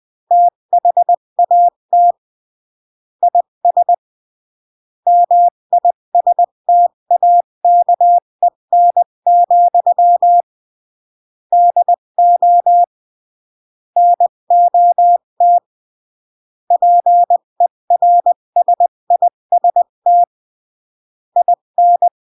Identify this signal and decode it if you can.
Morse code, heard as a keyed tone, about 20 wpm, 700 Hz. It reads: THAT IS MISTAKEN, DO NOT PERSIST IN
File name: CW / jodorowsky / 82 commandments 20-15